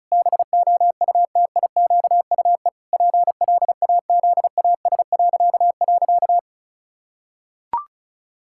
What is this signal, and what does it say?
Morse code, heard as a keyed tone, about 34 wpm, 700 Hz. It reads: BOUTIQUE PLAZAS..
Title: Pangram Challenge v2 - 35wpm